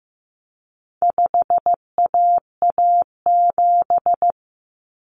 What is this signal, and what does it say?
Morse code, heard as a keyed tone, about 15 wpm, 700 Hz. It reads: 5AA7
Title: Common Cut Numbers - 4-Digit 15wpm